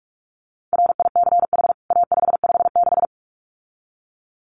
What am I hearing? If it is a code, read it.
Morse code, heard as a keyed tone, about 44 wpm, 700 Hz. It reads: RICH A556
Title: CWT Exchange 45wpm